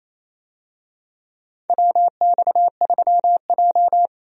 Morse code, heard as a keyed tone, about 28 wpm, 700 Hz. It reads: WX3J
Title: USA Call Signs - 4 Characters - Repeated 2x - 28wpm